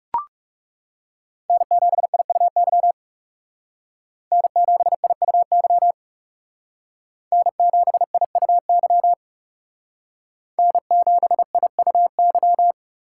Morse code, heard as a keyed tone, about 37 wpm, 700 Hz. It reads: N7IUY N7IUY N7IUY N7IUY
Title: Speed Racing - USA 1x3 Call Signs 30wpm